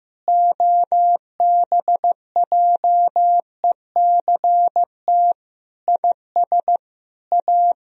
Morse code, heard as a keyed tone, about 15 wpm, 700 Hz. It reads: OBJECT IS A
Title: Sentences from Top 1000 Words - Review Part 3 - 15wpm